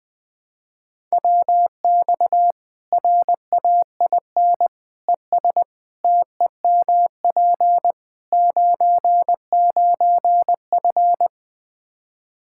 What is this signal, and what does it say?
Morse code, heard as a keyed tone, about 20 wpm, 700 Hz. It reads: WX RAIN ES TEMP 99F
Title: Practice QSO - Exchange 2 - Info - 20wpm